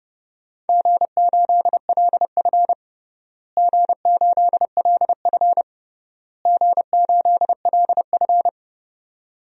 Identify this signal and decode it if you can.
Morse code, heard as a keyed tone, about 30 wpm, 700 Hz. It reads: G8LF G8LF G8LF
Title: DXCC-100 Call Signs - 4 Characters - Repeated 3x - 30wpm